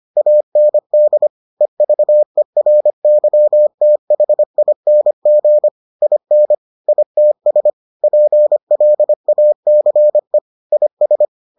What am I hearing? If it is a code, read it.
Morse code, heard as a keyed tone, about 25 wpm, 600 Hz. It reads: AND EVERYTHING IN ITS PLACE IS